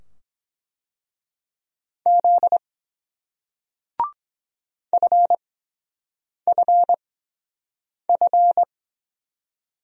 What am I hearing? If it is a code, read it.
Morse code, heard as a keyed tone, about 23 wpm, spaced as for 7 wpm, 700 Hz. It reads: Z FFF